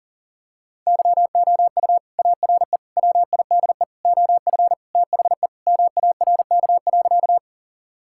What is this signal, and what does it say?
Morse code, heard as a keyed tone, about 40 wpm, 700 Hz. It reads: YOU ARE WIDE OF THE MARK.
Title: Sentences from Top 1000 Words - Review Part 6 - 40wpm